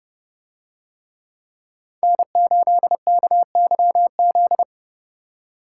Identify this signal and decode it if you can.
Morse code, heard as a keyed tone, about 29 wpm, 700 Hz. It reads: N8KYZ